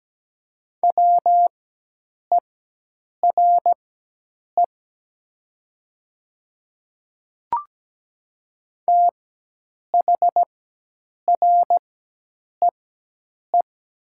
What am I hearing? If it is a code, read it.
Morse code, heard as a keyed tone, about 17 wpm, spaced as for 8 wpm, 700 Hz. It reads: WERE THREE
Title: 4x Character Spacing - Top 300 Words - 17wpm